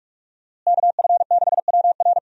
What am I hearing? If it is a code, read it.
Morse code, heard as a keyed tone, about 47 wpm, 700 Hz. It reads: KF6WR
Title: USA Call Signs - 5 Characters - Repeated 2x - 45wpm